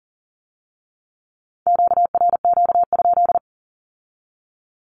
Morse code, heard as a keyed tone, about 40 wpm, 700 Hz. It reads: QRQ?
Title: Q-Codes with Spoken Meaning - 40wpm